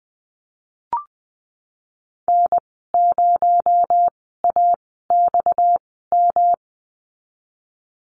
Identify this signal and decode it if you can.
Morse code, heard as a keyed tone, about 20 wpm, spaced as for 15 wpm, 700 Hz. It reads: N0AXM
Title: 245 Lesson 39g - Letter z - Call Signs 20wpm